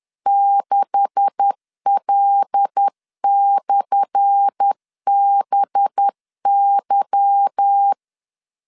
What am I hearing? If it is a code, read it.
Morse code, heard as a keyed tone, about 10 wpm, 800 Hz. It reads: 6L/BY
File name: LEZIONI / 10 SIGNS / 10A SLASH 10wpm 2808